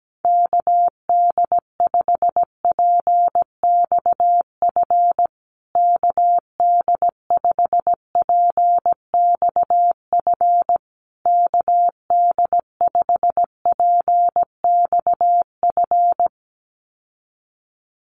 Morse code, heard as a keyed tone, about 17 wpm, 700 Hz. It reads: KD5PXF KD5PXF KD5PXF
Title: USA Call Signs Repeated 3x 17wpm